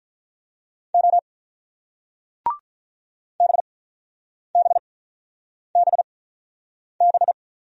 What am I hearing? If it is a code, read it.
Morse code, heard as a keyed tone, about 46 wpm, spaced as for 9 wpm, 700 Hz. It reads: O BBBB